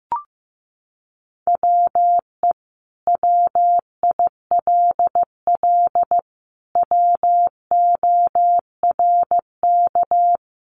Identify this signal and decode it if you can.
Morse code, heard as a keyed tone, about 15 wpm, 700 Hz. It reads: WE WILL WORK
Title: Sentences from Top 1100 Words 15wpm